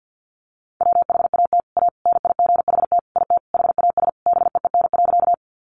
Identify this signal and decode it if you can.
Morse code, heard as a keyed tone, about 50 wpm, 700 Hz. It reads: WHAT A NIGHT IT HAS BEEN.
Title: Sentences from Top 300 Words 50wpm